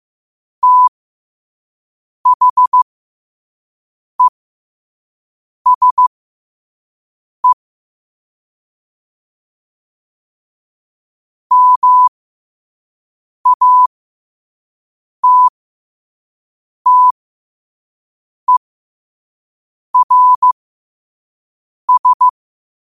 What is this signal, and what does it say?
Morse code, heard as a keyed tone, about 15 wpm, spaced as for 5 wpm, 1000 Hz. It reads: THESE MATTERS
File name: qotdfiles / 5wpm